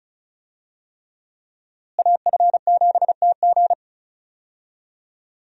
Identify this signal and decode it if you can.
Morse code, heard as a keyed tone, about 34 wpm, 700 Hz. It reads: AF7TG
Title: USA Call Signs - 5 Characters - Repeated 3x - 35wpm